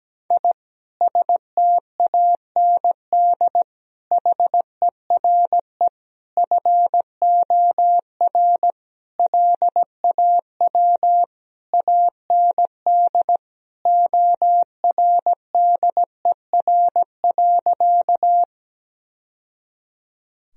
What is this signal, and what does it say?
Morse code, heard as a keyed tone, about 17 wpm, 700 Hz. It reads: I STAND HERE FOR LAW AND ORDER.
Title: Sentences from Top 1000 Words - Review Part 3 - 17wpm